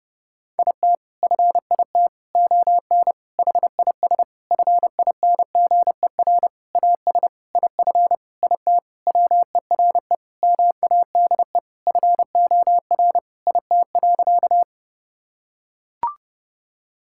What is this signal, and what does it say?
Morse code, heard as a keyed tone, about 30 wpm, 700 Hz. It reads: IT FIT ON HIS FINGER AS IF IT WERE MADE FOR IT.
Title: Sentences from Top 1600 Words 30wpm